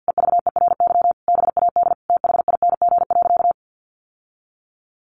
Morse code, heard as a keyed tone, about 50 wpm, 700 Hz. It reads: EVERY BAD THING.
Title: Sentences from Top 1200 Words 50wpm